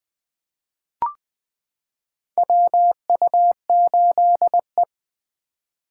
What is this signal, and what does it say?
Morse code, heard as a keyed tone, about 20 wpm, 700 Hz. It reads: WU8E
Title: 2023 POTA Hunter Call Signs - 20wpm